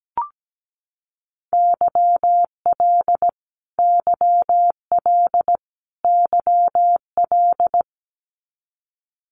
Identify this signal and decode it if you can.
Morse code, heard as a keyed tone, about 17 wpm, 700 Hz. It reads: YL YL YL
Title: Abbreviations Repeated 3x - 17wpm